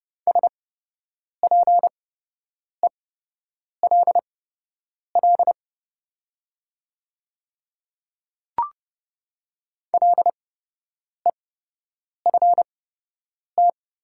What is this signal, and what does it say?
Morse code, heard as a keyed tone, about 30 wpm, spaced as for 8 wpm, 700 Hz. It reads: SPELL LEFT
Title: 8x Character Spacing - Top 300 Words - 30wpm